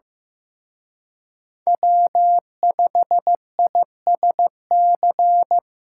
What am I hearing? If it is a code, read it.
Morse code, heard as a keyed tone, about 18 wpm, 700 Hz. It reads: W5ISC
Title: Speed Racing - USA 1x3 Call Signs 15wpm